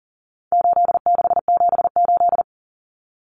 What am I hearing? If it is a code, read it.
Morse code, heard as a keyed tone, about 40 wpm, 700 Hz. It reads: 8678
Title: Numbers - 4-Digit 40wpm